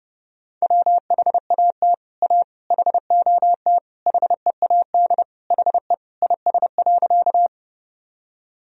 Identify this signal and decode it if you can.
Morse code, heard as a keyed tone, about 30 wpm, 700 Hz. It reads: WHAT A HOT HEAD HE IS.
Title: Sentences from Top 1000 Words - Review Part 3 - 30wpm